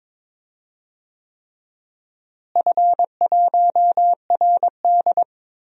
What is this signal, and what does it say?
Morse code, heard as a keyed tone, about 22 wpm, 700 Hz. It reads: F1RD
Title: DXCC-100 Call Signs - 4 Characters - Repeated 3x - 22wpm